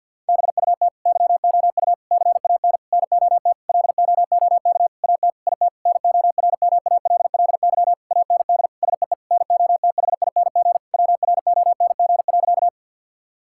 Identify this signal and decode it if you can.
Morse code, heard as a keyed tone, about 52 wpm, 700 Hz. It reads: BUT YOU CAN NOT LOOK AT IT NORMALLY AND SEE NOTHING WRONG.
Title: Sentences from Top 1000 Words - Review Part 6 - 50wpm